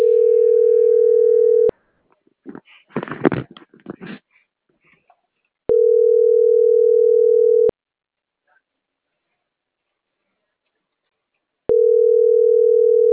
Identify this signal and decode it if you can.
Ringback tone (North America), 440 + 480 Hz.